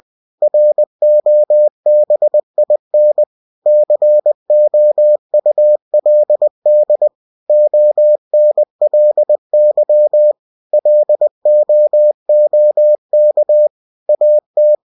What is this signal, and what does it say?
Morse code, heard as a keyed tone, about 20 wpm, 600 Hz. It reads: ROBIN COULD ONLY LOOK AT